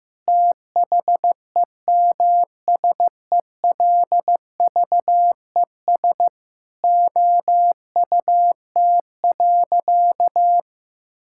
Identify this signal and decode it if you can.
Morse code, heard as a keyed tone, about 15 wpm, 700 Hz. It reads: THEMSELVES OUT.
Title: Sentences from Top 1000 Words - Review Part 5 - 15wpm